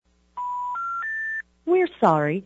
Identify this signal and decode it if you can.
Special information tone, with speech.
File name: 575-385-9999 valor